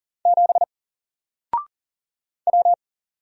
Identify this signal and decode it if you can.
Morse code, heard as a keyed tone, about 40 wpm, 700 Hz. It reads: 7 W
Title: Single Letter-Number Alt 40wpm